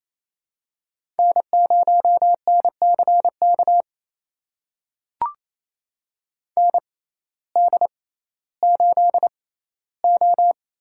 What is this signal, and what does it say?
Morse code, heard as a keyed tone, about 28 wpm, spaced as for 15 wpm, 700 Hz. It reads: N0NCK ND8O
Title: USA Call Signs - 6x Character Spacing - 28wpm